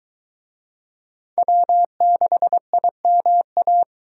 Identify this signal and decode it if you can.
Morse code, heard as a keyed tone, about 23 wpm, 700 Hz. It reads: W6IMA